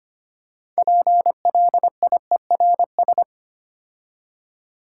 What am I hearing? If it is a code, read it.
Morse code, heard as a keyed tone, about 25 wpm, 700 Hz. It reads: PLIERS
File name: practice / 25wpm / Difficult Words with Hints - 25wpm